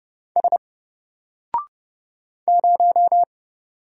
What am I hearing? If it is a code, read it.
Morse code, heard as a keyed tone, about 30 wpm, 700 Hz. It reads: S 0
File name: Single-Letter-Number-30wpm